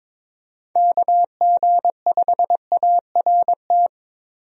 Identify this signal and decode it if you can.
Morse code, heard as a keyed tone, about 22 wpm, 700 Hz. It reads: KG5ART
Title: USA Call Signs - 6 Characters - Repeated 2x - 22wpm